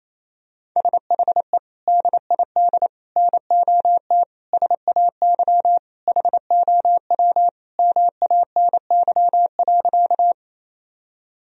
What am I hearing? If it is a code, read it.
Morse code, heard as a keyed tone, about 28 wpm, 700 Hz. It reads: SHE DID NOT SAY HOW MANY.
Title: Sentences from Top 1000 Words - Review Part 1 - 28wpm